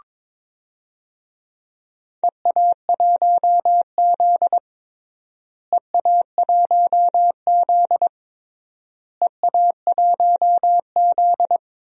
Morse code, heard as a keyed tone, about 22 wpm, 700 Hz. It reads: EA1Z EA1Z EA1Z
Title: DXCC-100 Call Signs - 4 Characters - Repeated 3x - 22wpm